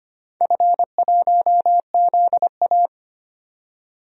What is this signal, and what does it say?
Morse code, heard as a keyed tone, about 25 wpm, 700 Hz. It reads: F1ZA
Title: DXCC-100 Call Signs - 4 Characters - 25wpm